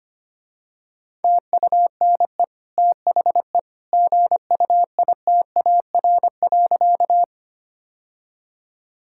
Morse code, heard as a keyed tone, about 25 wpm, 700 Hz. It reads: TUNE THE GUITAR.
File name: Easy Sentences with Hints - 25wpm